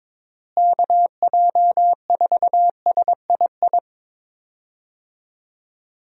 Morse code, heard as a keyed tone, about 22 wpm, 700 Hz. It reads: KJ4SII
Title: USA Call Signs - 6 Characters - Repeated 2x - 22wpm